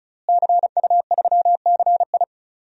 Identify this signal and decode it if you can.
Morse code, heard as a keyed tone, about 36 wpm, 700 Hz. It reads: CU3CI
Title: DXCC-100 Call Signs - 5 Characters - Repeated 2x - 35wpm